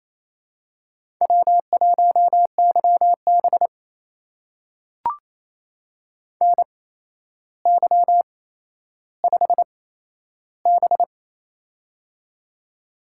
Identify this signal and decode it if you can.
Morse code, heard as a keyed tone, about 27 wpm, spaced as for 12 wpm, 700 Hz. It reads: W1YB NY5B
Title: USA Call Signs - 4 Characters - 8x Character Spacing - 28wpm